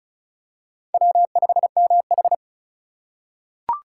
Morse code, heard as a keyed tone, about 36 wpm, 700 Hz. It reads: W5MH
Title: USA Call Signs - 4 Characters - 4x Character Spacing - 35wpm